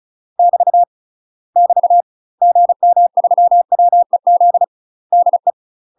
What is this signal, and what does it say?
Morse code, heard as a keyed tone, about 36 wpm, 700 Hz. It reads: <BT> <BT> GM3WEZ DE
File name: UK MASTER QSOs TRAINING / 35WPM / UK MASTER QSOs 0026